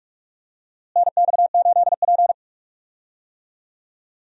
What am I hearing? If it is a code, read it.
Morse code, heard as a keyed tone, about 41 wpm, 700 Hz. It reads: NX8P
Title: USA Call Signs - 4 Characters - Repeated 2x - 45wpm